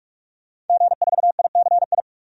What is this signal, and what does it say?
Morse code, heard as a keyed tone, about 46 wpm, 700 Hz. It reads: G4ICI
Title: DXCC-100 Call Signs - 5 Characters - 45wpm